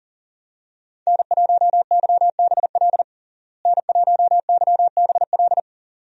Morse code, heard as a keyed tone, about 40 wpm, 700 Hz. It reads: N1YBL N1YBL
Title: USA Call Signs - 5 Characters - Repeated 2x - 40wpm